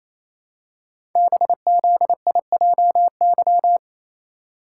Morse code, heard as a keyed tone, about 29 wpm, 700 Hz. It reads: BZIJY
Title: Code Groups - 5 Letters - 28wpm